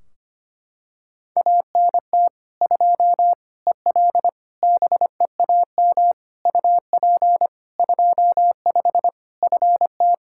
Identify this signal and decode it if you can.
Morse code, heard as a keyed tone, about 25 wpm, 700 Hz. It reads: ANT 2 EL BEAM UP 25 FT
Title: Practice QSO - Exchange 2 - Info - 25wpm